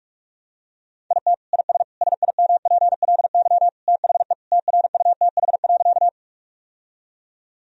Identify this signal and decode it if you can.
Morse code, heard as a keyed tone, about 45 wpm, 700 Hz. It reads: IT IS SIMPLY THE TRUTH.
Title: Sentences from Top 1000 Words - Review Part 4 - 45wpm